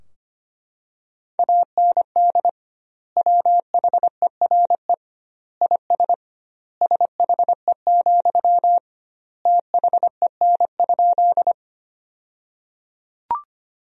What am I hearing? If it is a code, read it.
Morse code, heard as a keyed tone, about 25 wpm, 700 Hz. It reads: AND WHERE IS SHE, THEN?
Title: 2x Word-Spacing - Sentences from Top 200 Words - 25wpm